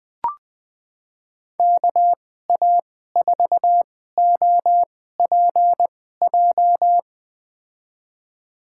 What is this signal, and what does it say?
Morse code, heard as a keyed tone, about 20 wpm, spaced as for 14 wpm, 700 Hz. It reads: KA4OPJ
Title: USA Call Signs - 6 Characters - 2x Character Spacing - 20wpm